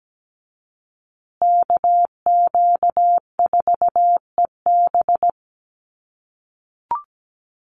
Morse code, heard as a keyed tone, about 17 wpm, 700 Hz. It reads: KQ4EB
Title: USA Call Signs Repeated 3x 17wpm